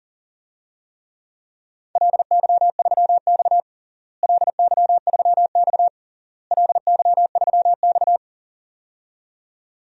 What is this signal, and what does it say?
Morse code, heard as a keyed tone, about 40 wpm, 700 Hz. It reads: LY3X LY3X LY3X
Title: DXCC-100 Call Signs - 4 Characters - Repeated 3x - 40wpm